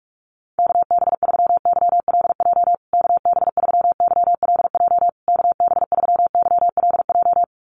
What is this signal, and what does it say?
Morse code, heard as a keyed tone, about 44 wpm, 700 Hz. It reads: KB3YLJ KB3YLJ KB3YLJ
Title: USA Call Signs Repeated 3x 45wpm